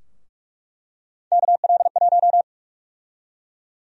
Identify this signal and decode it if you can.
Morse code, heard as a keyed tone, about 45 wpm, 700 Hz. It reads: KL1
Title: USA Call Sign Prefixes - Repeated 2x - 45wpm